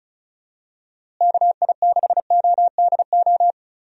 Morse code, heard as a keyed tone, about 36 wpm, 700 Hz. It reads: KI6ODO
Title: USA Call Signs - 6 Characters - Repeated 3x - 35wpm